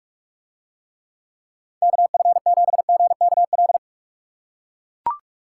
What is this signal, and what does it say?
Morse code, heard as a keyed tone, about 42 wpm, 700 Hz. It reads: KF7GKL